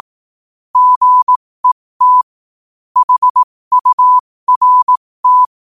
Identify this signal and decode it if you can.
Morse code, heard as a keyed tone, about 18 wpm, spaced as for 15 wpm, 1000 Hz. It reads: GET HURT